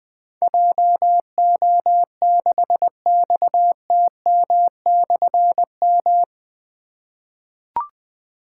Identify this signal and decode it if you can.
Morse code, heard as a keyed tone, about 20 wpm, 700 Hz. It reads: JO6XTM/M
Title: Difficult Call Signs 20wpm